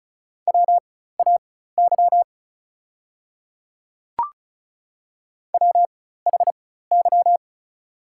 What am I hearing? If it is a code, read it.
Morse code, heard as a keyed tone, about 36 wpm, spaced as for 16 wpm, 690 Hz. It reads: WAY WHY
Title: 4x Character Spacing - Top 500 Words - 35wpm